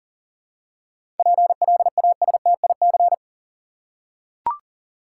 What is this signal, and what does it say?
Morse code, heard as a keyed tone, about 40 wpm, 700 Hz. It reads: PLASTIC
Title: Seven Letter Words - 40wpm